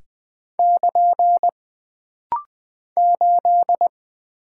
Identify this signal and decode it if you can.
Morse code, heard as a keyed tone, about 20 wpm, 700 Hz. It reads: <KN> 8
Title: Single Character (Letter:Number:Punctuation:Prosign) - Rapid-Fire - 20wpm